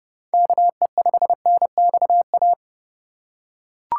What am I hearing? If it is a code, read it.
Morse code, heard as a keyed tone, about 32 wpm, 700 Hz. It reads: KE5NXA
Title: USA Call Signs - 6 Characters - 2x Character Spacing - 30wpm